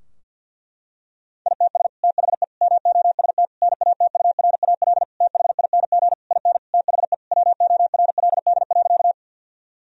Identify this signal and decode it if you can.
Morse code, heard as a keyed tone, about 50 wpm, 700 Hz. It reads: ITS THE MOST NATURAL THING IN THE WORLD.
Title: Sentences from Top 1000 Words - Review Part 4 - 50wpm